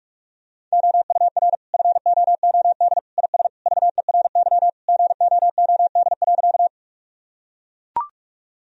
Morse code, heard as a keyed tone, about 41 wpm, 700 Hz. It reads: OUR FOOD IS VERY GOOD.
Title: Sentences from Top 1000 Words - Review Part 1 - 45wpm